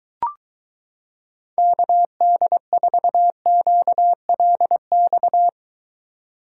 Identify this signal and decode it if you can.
Morse code, heard as a keyed tone, about 23 wpm, 700 Hz. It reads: KD4QLX